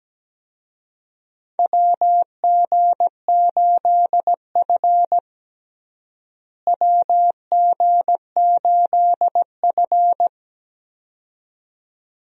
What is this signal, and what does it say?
Morse code, heard as a keyed tone, about 17 wpm, 700 Hz. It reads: WG8F WG8F
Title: USA Call Signs - 4 Characters - Repeated 2x - 17wpm